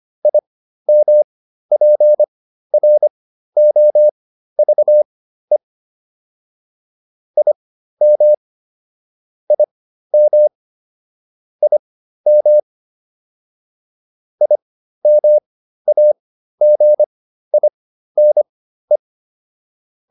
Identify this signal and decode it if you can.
Morse code, heard as a keyed tone, about 25 wpm, spaced as for 13 wpm, 600 Hz. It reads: IMPROVE IM IM IM IMAGINE
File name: IM7 13